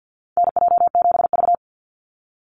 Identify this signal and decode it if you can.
Morse code, heard as a keyed tone, about 50 wpm, 700 Hz. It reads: NJ7V
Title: NA SOTA Chaser Callsigns 50wpm